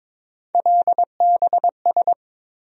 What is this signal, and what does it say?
Morse code, heard as a keyed tone, about 21 wpm, 700 Hz. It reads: LBS
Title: USA Call Sign Postfixes - 3 Letters 22wpm